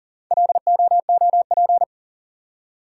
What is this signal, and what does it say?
Morse code, heard as a keyed tone, about 41 wpm, 700 Hz. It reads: LOOP
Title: Top 180 QSO Elements 40wpm